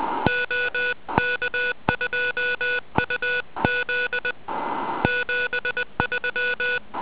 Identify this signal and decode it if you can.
Morse code, heard as a keyed tone, about 20 wpm, 1470 Hz. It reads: OK2UZ 73
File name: ARCHIV / zvuk1 / DL2DZL